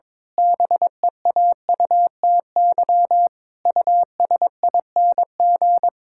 Morse code, heard as a keyed tone, about 22 wpm, 700 Hz. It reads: BEAUTY USING
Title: Pangram Challenge v2 - 22wpm